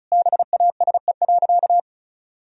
Morse code, heard as a keyed tone, about 36 wpm, 700 Hz. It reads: BASE.
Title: Sentences from Top 1000 Words - Review Part 1 - 35wpm